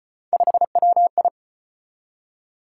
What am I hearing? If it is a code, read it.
Morse code, heard as a keyed tone, about 34 wpm, 700 Hz. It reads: 5WI